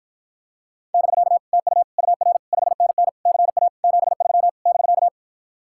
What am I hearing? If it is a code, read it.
Morse code, heard as a keyed tone, about 54 wpm, 700 Hz. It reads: <BK> TU UR 5NN CA 73 <BK>